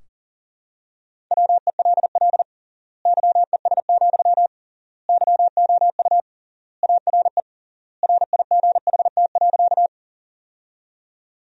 Morse code, heard as a keyed tone, about 40 wpm, 700 Hz. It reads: WELL YES, YOU ARE RIGHT.
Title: 3x Word-Spacing - Sentences from Top 500 Words - 40wpm